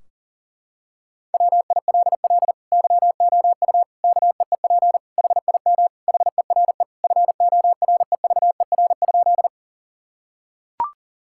Morse code, heard as a keyed tone, about 40 wpm, 700 Hz. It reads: WILL YOU KEEP HIM HERE FOREVER?
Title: Sentences from Top 1600 Words 40wpm